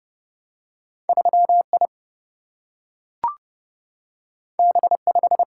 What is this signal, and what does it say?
Morse code, heard as a keyed tone, about 31 wpm, 700 Hz. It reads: 3 I B 5
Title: Single-Letter Single-Number 30wpm